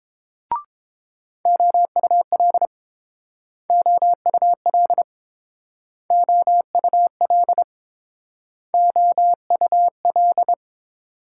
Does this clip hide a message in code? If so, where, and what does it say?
Yes: Morse code from 1.5 to 10.5 s, heard as a keyed tone, about 28 wpm, 700 Hz. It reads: OUL OUL OUL OUL